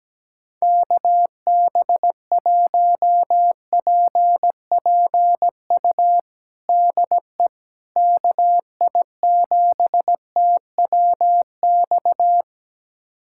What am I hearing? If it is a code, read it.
Morse code, heard as a keyed tone, about 17 wpm, 700 Hz. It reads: KB1PPU DE KI7TWX
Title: USA Callsign DE Callsign - 6 Characters - 17wpm